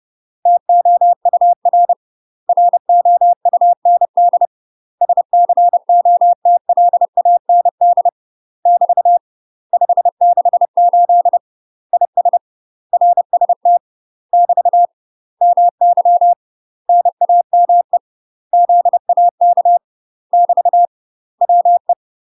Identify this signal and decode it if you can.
Morse code, heard as a keyed tone, about 30 wpm, 700 Hz. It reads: TOUR ROUND SCOTLAND <BT> 568 IS RST <BT> MY NAME ZAK <BT> WE
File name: UK MASTER QSOs TRAINING / 30WPM / UK MASTER QSOs 0042